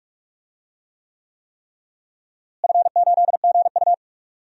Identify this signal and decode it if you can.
Morse code, heard as a keyed tone, about 46 wpm, 700 Hz. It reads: F8GU
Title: DXCC-100 Call Signs - 4 Characters - Repeated 3x - 45wpm